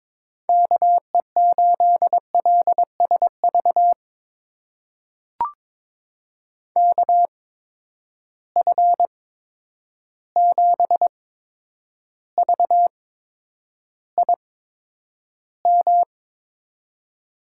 Morse code, heard as a keyed tone, about 22 wpm, spaced as for 9 wpm, 700 Hz. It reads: KE8LSV KF7VIM
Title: USA Call Signs - 6 Characters - 8x Character Spacing - 22wpm